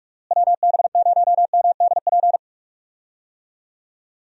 Morse code, heard as a keyed tone, about 43 wpm, 700 Hz. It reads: WB0MDP
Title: USA Call Signs - 6 Characters - Repeated 3x - 45wpm